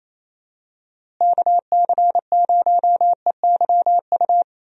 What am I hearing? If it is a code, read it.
Morse code, heard as a keyed tone, about 28 wpm, 700 Hz. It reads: KC0EYU
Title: USA Call Signs - 6 Characters - Repeated 3x - 28wpm